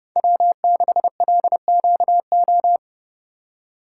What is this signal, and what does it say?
Morse code, heard as a keyed tone, about 32 wpm, 700 Hz. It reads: W6LQO